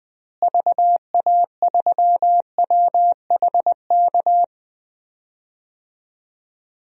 Morse code, heard as a keyed tone, about 20 wpm, 700 Hz. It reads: VA3WHK
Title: DXCC-100 Call Signs - 6 Characters - Repeated 2x - 20wpm